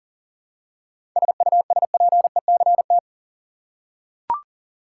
Morse code, heard as a keyed tone, about 40 wpm, 700 Hz. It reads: SUSPECT
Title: Seven Letter Words - 40wpm